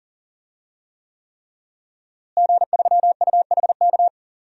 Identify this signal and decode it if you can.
Morse code, heard as a keyed tone, about 40 wpm, 700 Hz. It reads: G3UHK
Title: DXCC-100 Call Signs - 5 Characters - Repeated 3x - 40wpm